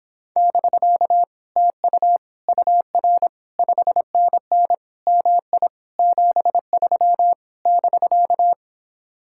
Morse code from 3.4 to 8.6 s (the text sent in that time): 5NN MI 73 <BK>